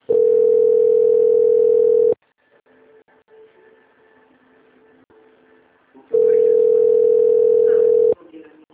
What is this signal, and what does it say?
Ringback tone (North America), 440 + 480 Hz, with speech.
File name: out- - - - - .62486